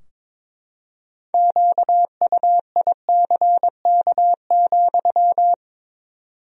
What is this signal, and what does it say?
Morse code, heard as a keyed tone, about 22 wpm, 700 Hz. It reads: QUICK,
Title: 4x Word-Spacing - Sentences from Top 500 Words - 22wpm